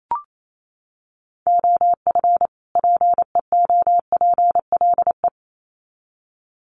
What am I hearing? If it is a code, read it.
Morse code, heard as a keyed tone, about 28 wpm, 700 Hz. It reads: OF PEOPLE
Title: Sets of 2 Words 28wpm